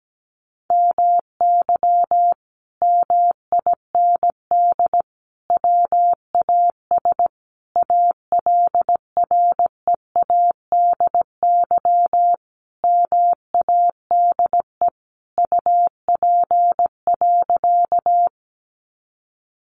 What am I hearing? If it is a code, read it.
Morse code, heard as a keyed tone, about 17 wpm, 700 Hz. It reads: MY MIND WAS ALREADY MADE UP.
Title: Sentences from Top 700 Words 17wpm